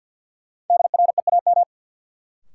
Morse code, heard as a keyed tone, about 48 wpm, 700 Hz. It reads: DREAM
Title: Sentences from Top 1500 Words - New Word Drill - 50wpm